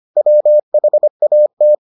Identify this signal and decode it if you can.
Morse code, heard as a keyed tone, about 25 wpm, 600 Hz. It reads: WHAT